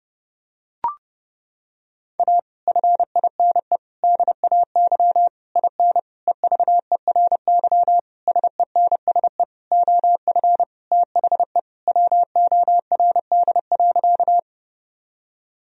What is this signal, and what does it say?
Morse code, heard as a keyed tone, about 30 wpm, 700 Hz. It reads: A FINE DAY IN EVERY SENSE OF THE WORD.
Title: Sentences from Top 1000 Words - Review Part 4 - 30wpm